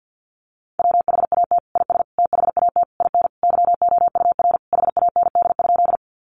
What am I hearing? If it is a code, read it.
Morse code, heard as a keyed tone, about 50 wpm, 700 Hz. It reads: WHAT IS THAT IN YOUR HAND?
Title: Sentences from Top 200 Words 50wpm